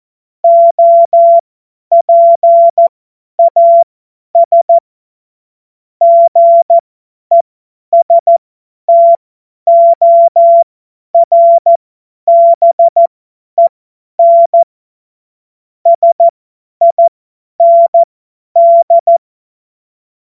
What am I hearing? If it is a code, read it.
Morse code, heard as a keyed tone, about 14 wpm, spaced as for 10 wpm, 680 Hz. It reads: OPAS GESTORBEN SIND
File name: Morsekurs / Witze / Witz 26. Oktober 2021, 50 F70